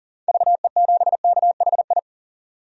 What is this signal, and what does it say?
Morse code, heard as a keyed tone, about 40 wpm, 700 Hz. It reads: VE7KHI
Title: DXCC-100 Call Signs - 6 Characters - 40wpm